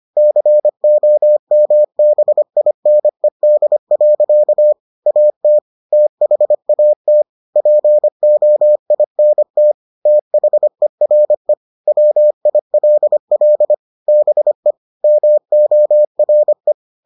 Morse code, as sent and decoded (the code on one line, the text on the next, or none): -.-. --- -- -... .. -. -.. .-.-.- / .- - / - .... .- - / .--. --- .. -. - / - .... .-. / .-- .. .-.. .-.. / -... / -- --- .-. .
COMBINED. AT THAT POINT THERE WILL BE MORE